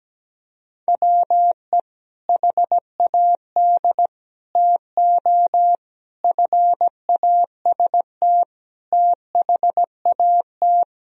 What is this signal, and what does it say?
Morse code, heard as a keyed tone, about 17 wpm, 700 Hz. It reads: WE HAD TO FAST THAT
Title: Sentences from Top 1000 Words - Review Part 2 - 17wpm